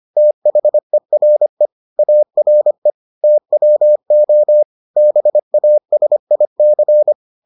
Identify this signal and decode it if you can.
Morse code, heard as a keyed tone, about 25 wpm, 600 Hz. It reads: THERE ARE TWO BASIC